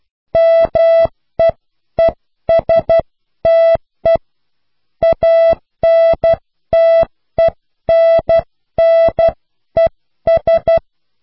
Morse code, heard as a keyed tone, about 12 wpm, 660 Hz. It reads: MEESTE ANTENNES